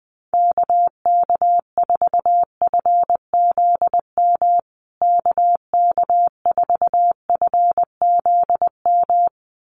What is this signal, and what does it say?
Morse code, heard as a keyed tone, about 20 wpm, 700 Hz. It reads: KK4FZM KK4FZM
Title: USA Call Signs Repeated 2x 20wpm